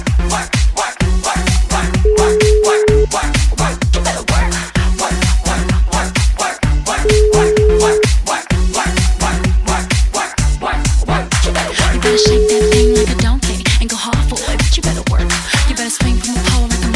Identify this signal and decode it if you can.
Ringback tone (Europe, 425 Hz), with noise.